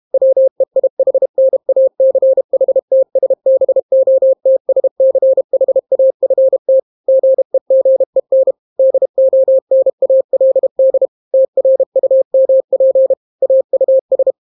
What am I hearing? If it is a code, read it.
Morse code, heard as a keyed tone, about 32 wpm, 530 Hz. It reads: WEIHNACHTSBOTSCHAFT GEGEN DONALD TRUMP AUS